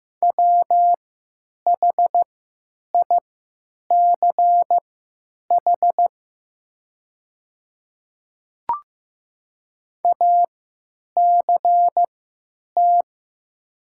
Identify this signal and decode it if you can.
Morse code, heard as a keyed tone, about 15 wpm, spaced as for 9 wpm, 700 Hz. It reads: WHICH ACT
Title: 3x Character Spacing - Top 200 Words - 15wpm